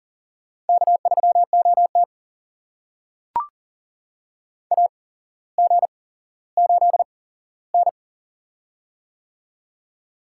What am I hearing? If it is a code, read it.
Morse code, heard as a keyed tone, about 40 wpm, spaced as for 17 wpm, 700 Hz. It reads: K3OT AG8N
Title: USA Call Signs - 4 Characters - 8x Character Spacing - 40wpm